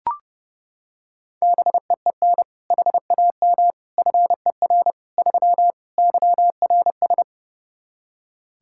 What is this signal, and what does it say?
Morse code, heard as a keyed tone, about 30 wpm, 700 Hz. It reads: BEEN HAM FER 3 YRS
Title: Practice QSO - Exchange 3 - More Info - 30wpm